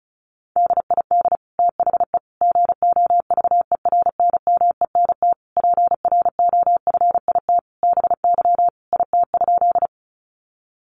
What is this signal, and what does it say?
Morse code, heard as a keyed tone, about 34 wpm, 700 Hz. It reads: DID THE GOVERNMENT PROFIT BY IT?
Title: Sentences from Top 800 Words 35wpm